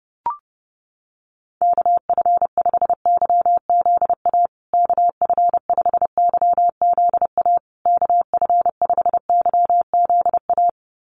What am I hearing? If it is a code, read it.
Morse code, heard as a keyed tone, about 30 wpm, 700 Hz. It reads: KF5YZA KF5YZA KF5YZA
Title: USA Call Signs Repeated 3x 30wpm